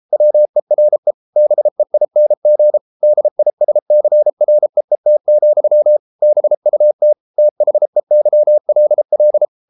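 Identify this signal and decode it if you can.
Morse code, heard as a keyed tone, about 33 wpm, 600 Hz. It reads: WERE BEING DISCREET, BUT THEYLL
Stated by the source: Morse code practice at 32 wpm from CW med Gnister, reading Ken Follett's novel Never in English.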